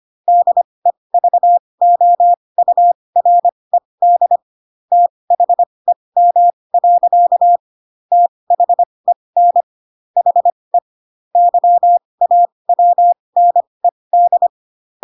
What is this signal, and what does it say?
Morse code, heard as a keyed tone, about 25 wpm, spaced as for 20 wpm, 700 Hz. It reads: DEVOURED THEM. THEN HE YAWNED
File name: Pinocchio / Pinocchio ch7 15 25 ik1hge